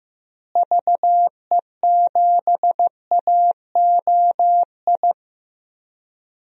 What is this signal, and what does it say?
Morse code, heard as a keyed tone, about 15 wpm, 700 Hz. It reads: VE7AOI
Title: DXCC-100 Call Signs - 6 Characters - Repeated 3x - 15wpm